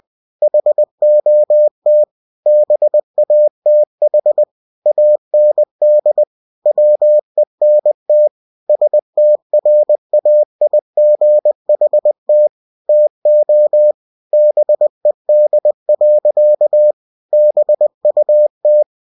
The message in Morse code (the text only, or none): HOT BATH AND WENT STRAIGHT TO BED. BUT